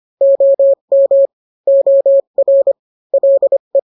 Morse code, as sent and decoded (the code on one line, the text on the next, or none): --- -- / --- .-. / .-.. .
OM OR LE